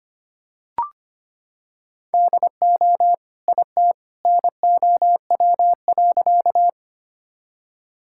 Morse code, heard as a keyed tone, about 25 wpm, 700 Hz. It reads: DO IT NOW.
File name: practice / 25wpm / Sentences with 2-Word Phrases - 25wpm